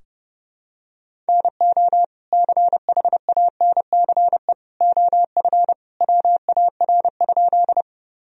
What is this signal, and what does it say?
Morse code, heard as a keyed tone, about 32 wpm, 700 Hz. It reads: NO CHANCE OF WAR?
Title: Sentences from Top 1000 Words - Review Part 2 - 30wpm